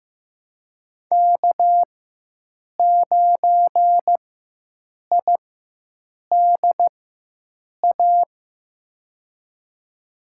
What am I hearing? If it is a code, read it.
Morse code, heard as a keyed tone, about 15 wpm, spaced as for 7 wpm, 700 Hz. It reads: K9IDA
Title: USA Call Signs - 4x Character Spacing - 15wpm